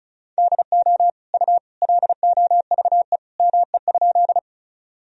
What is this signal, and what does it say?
Morse code, heard as a keyed tone, about 36 wpm, 700 Hz. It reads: DO U LOVE ME?
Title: Sentences from Top 400 Words and Abbreviations 35wpm